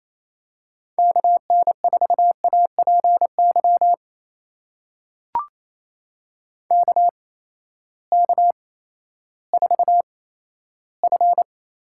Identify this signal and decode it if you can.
Morse code, heard as a keyed tone, about 28 wpm, spaced as for 14 wpm, 700 Hz. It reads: KN4APY KK4F
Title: USA Call Signs - 6 Characters - 8x Character Spacing - 28wpm